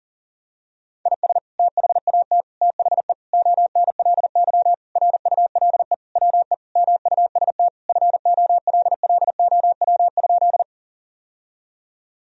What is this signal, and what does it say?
Morse code, heard as a keyed tone, about 40 wpm, 700 Hz. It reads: IS THAT THE ONLY RULE WE MUST FOLLOW?
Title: Sentences from Top 1000 Words - Review Part 2 - 40wpm